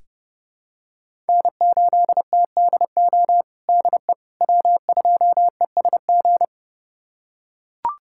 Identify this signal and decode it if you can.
Morse code, heard as a keyed tone, about 30 wpm, 700 Hz. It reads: N8TDO DE W2ESG
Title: USA Callsign DE Callsign - 5 Characters - 30wpm